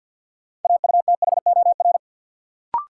Morse code, heard as a keyed tone, about 53 wpm, 700 Hz. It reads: AUTHOR
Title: Sentences from Top 1200 Words - New Word Drill Alt - 50wpm